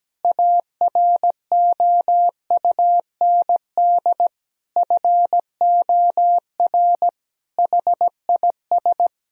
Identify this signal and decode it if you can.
Morse code, heard as a keyed tone, about 17 wpm, 700 Hz. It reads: AROUND FOR HIS
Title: Sentences from Top 1000 Words - Review Part 3 - 17wpm